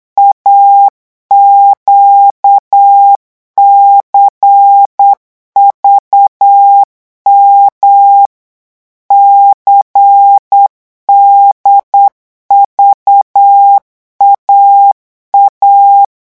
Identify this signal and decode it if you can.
Morse code, heard as a keyed tone, about 8 wpm, 800 Hz. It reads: AQCVM CDVAA
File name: morse mp3 / revision3-vitesse 8 mots